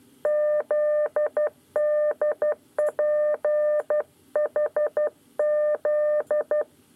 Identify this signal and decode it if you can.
Morse code, heard as a keyed tone, about 12 wpm, 600 Hz. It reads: ZDPHZ
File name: 9 lezione cw 2